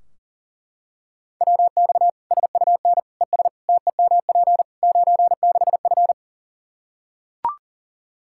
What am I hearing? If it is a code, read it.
Morse code, heard as a keyed tone, about 40 wpm, 700 Hz. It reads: WX SUN ES TEMP 96F
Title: Practice QSO - Exchange 2 - Info - 40wpm